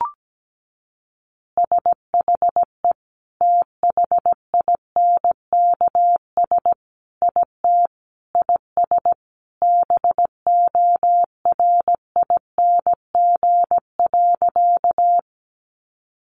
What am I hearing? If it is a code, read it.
Morse code, heard as a keyed tone, about 17 wpm, 700 Hz. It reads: SHE THINKS IT IS BORING.
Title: Easy Phrases 17wpm